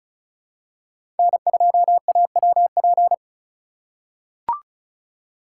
Morse code, heard as a keyed tone, about 36 wpm, 700 Hz. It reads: N2AWP
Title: USA Call Signs - 5 Characters - 4x Character Spacing - 35wpm